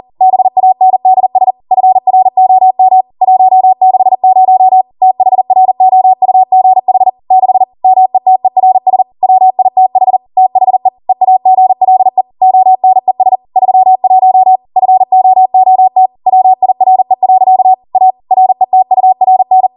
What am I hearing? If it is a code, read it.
Morse code, heard as a keyed tone, about 40 wpm, 750 Hz. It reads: BANDS FROM 160 THROUGH 6 METERS WITH THE EAGLE ONES 31 FOOT WIRE. A RETURN